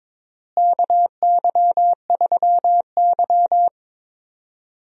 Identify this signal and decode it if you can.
Morse code, heard as a keyed tone, about 21 wpm, 700 Hz. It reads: KY3Y
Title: USA Call Signs - 4 Characters - Repeated 2x - 22wpm